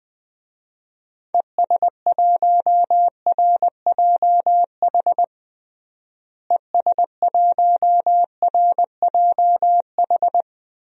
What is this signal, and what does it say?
Morse code, heard as a keyed tone, about 21 wpm, 700 Hz. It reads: ES1RJH ES1RJH
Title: DXCC-100 Call Signs - 6 Characters - Repeated 2x - 20wpm